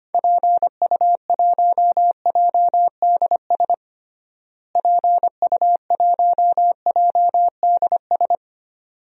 Morse code, heard as a keyed tone, about 25 wpm, 700 Hz. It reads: PU1JDS PU1JDS
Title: DXCC-100 Call Signs - 6 Characters - Repeated 2x - 25wpm